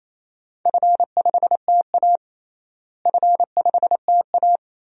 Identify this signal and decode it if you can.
Morse code, heard as a keyed tone, about 28 wpm, 700 Hz. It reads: F5TA F5TA
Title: DXCC-100 Call Signs - 4 Characters - Repeated 2x - 28wpm